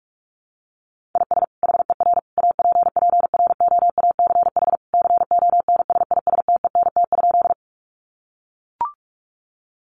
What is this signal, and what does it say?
Morse code, heard as a keyed tone, about 44 wpm, 700 Hz. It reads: IS HER APPROACH CONSISTENT?
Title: Sentences from Top 1500 Words 45wpm